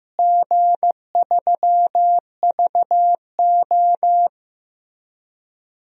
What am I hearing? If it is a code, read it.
Morse code, heard as a keyed tone, about 15 wpm, 700 Hz. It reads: G3VO
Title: DXCC-100 Call Signs - 4 Characters - Repeated 3x - 15wpm